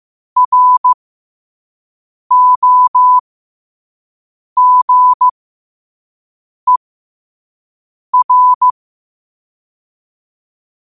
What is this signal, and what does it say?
Morse code, heard as a keyed tone, about 15 wpm, spaced as for 5 wpm, 1000 Hz. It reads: ROGER
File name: qotdfiles / 5wpm